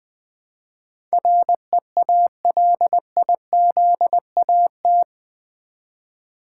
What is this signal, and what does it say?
Morse code, heard as a keyed tone, about 20 wpm, 700 Hz. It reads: REALIZAT
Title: Build A Word - 4 Letters At A Time - 11-Letter Words - 20wpm